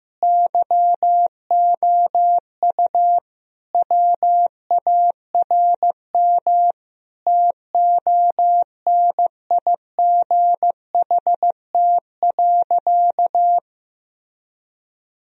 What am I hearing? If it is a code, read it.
Morse code, heard as a keyed tone, about 15 wpm, 700 Hz. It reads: YOU WARM TONIGHT.
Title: Sentences from Top 1000 Words - Review Part 3 - 15wpm